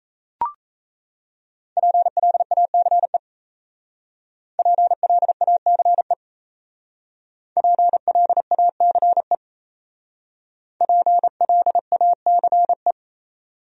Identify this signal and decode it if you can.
Morse code, heard as a keyed tone, about 34 wpm, 700 Hz. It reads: PLACE PLACE PLACE PLACE
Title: Speed-Racing Top 200 Words 28wpm